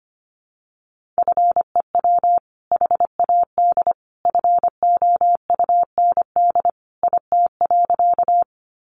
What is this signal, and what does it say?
Morse code, heard as a keyed tone, about 25 wpm, 700 Hz. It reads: FEW HAD FOUND IT.
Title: Sentences from Top 300 Words 25wpm